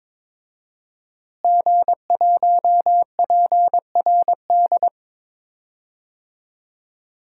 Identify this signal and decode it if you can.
Morse code, heard as a keyed tone, about 21 wpm, 700 Hz. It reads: G1PRD
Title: DXCC-100 Call Signs - 5 Characters - Repeated 2x - 22wpm